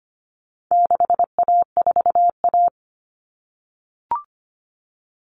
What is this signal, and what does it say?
Morse code, heard as a keyed tone, about 25 wpm, 700 Hz. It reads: 6A4A
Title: Common Cut Numbers - 4-Digit 25wpm